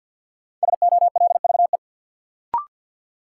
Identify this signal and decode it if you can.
Morse code, heard as a keyed tone, about 46 wpm, 700 Hz. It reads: SOLVE
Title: Sentences from Top 1000 Words - New Word Drill - 50wpm